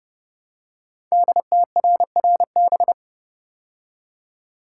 Morse code, heard as a keyed tone, about 30 wpm, 700 Hz. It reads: DTRRB
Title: Code Groups - 5 Letters - 30wpm